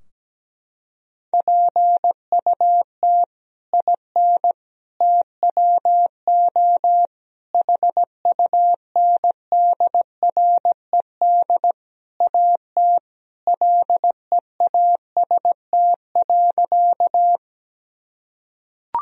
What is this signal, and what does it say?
Morse code, heard as a keyed tone, about 17 wpm, 700 Hz. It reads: PUT IN TWO HUNDRED AT LEAST.
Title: Sentences from Top 1000 Words - Review Part 2 - 17wpm